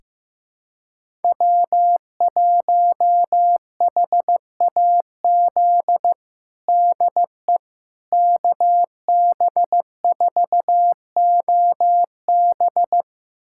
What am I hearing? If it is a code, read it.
Morse code, heard as a keyed tone, about 15 wpm, 700 Hz. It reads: W1HAZ DE KB4OB
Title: USA Callsign DE Callsign - 5 Characters - 15wpm